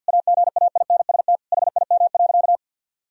Morse code, heard as a keyed tone, about 50 wpm, 700 Hz. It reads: AGAINST HIM.
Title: Sentences from Top 1000 Words - Review Part 3 - 50wpm